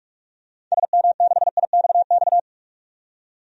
Morse code, heard as a keyed tone, about 47 wpm, 700 Hz. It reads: SM6IXX
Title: DXCC-100 Call Signs - 6 Characters - 45wpm